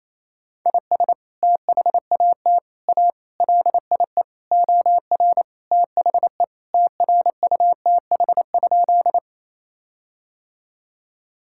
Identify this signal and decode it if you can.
Morse code, heard as a keyed tone, about 27 wpm, 700 Hz. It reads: IS THAT A LIE OR THE TRUTH?
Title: Sentences from Top 1600 Words 28wpm